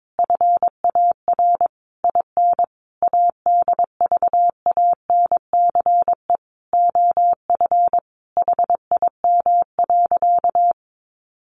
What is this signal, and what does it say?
Morse code, heard as a keyed tone, about 22 wpm, 700 Hz. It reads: FAR IN ADVANCE OF HIM.
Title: Sentences from Top 1400 Words 22wpm